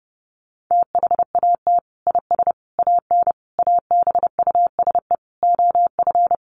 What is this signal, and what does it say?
Morse code, heard as a keyed tone, about 30 wpm, 700 Hz. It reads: THAT IS AN ABUSE OF
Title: Sentences from Top 1300 Words 30wpm